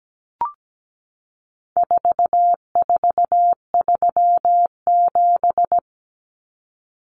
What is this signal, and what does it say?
Morse code, heard as a keyed tone, about 17 wpm, 700 Hz. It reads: 4437
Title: Numbers - 4-Digit 17wpm